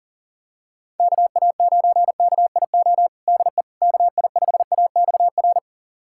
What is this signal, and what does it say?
Morse code, heard as a keyed tone, about 40 wpm, 700 Hz. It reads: KA9KIO DE KI5AXR
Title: USA Callsign DE Callsign - 6 Characters - 40wpm